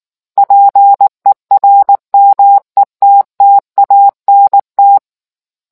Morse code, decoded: PERMETTANT

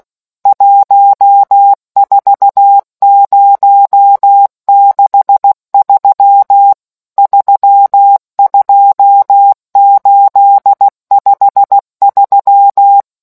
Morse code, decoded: 14063 32853

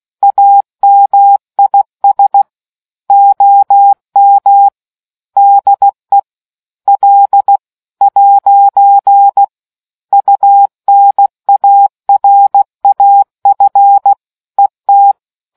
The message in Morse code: AMIS OM DE L'UNARAF ET

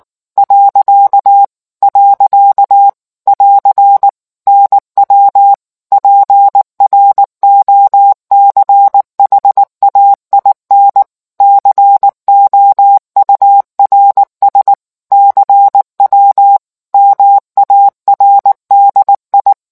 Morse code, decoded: . <AR> NW PROCHAIN COURS CW MARDI

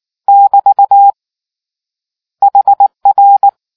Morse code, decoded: <BT> HR

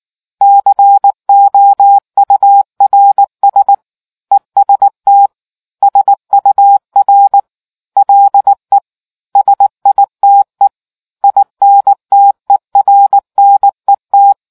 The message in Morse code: COURS EST SUR LE SITE INTERNET

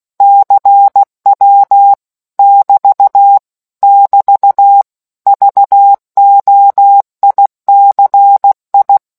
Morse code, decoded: CW <BT> <BT> VOICI

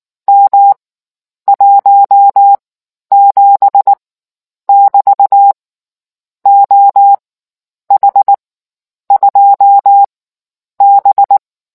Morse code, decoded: M 17 <BT> OH2B

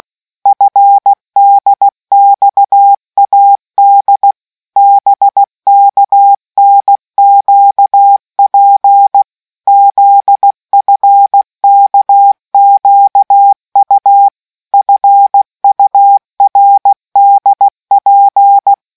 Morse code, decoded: FDXAD BKNQP ZFKQU FURDP

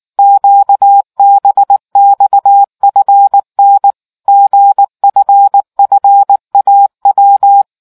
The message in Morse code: QBXFN GFFAW

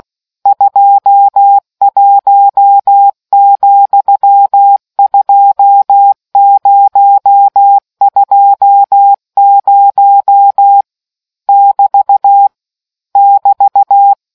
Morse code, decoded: 21,2020 <BT> <BT>

